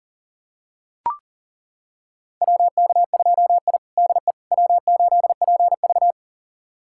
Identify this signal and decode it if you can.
Morse code, heard as a keyed tone, about 37 wpm, 700 Hz. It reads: WK2I DE W8PV
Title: USA Callsign DE Callsign - 4 Characters - 40wpm